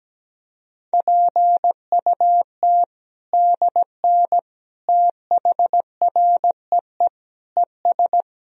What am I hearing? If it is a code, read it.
Morse code, heard as a keyed tone, about 17 wpm, 700 Hz. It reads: PUT DN THREE ES
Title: Sentences from Top 400 Words and Abbreviations 17wpm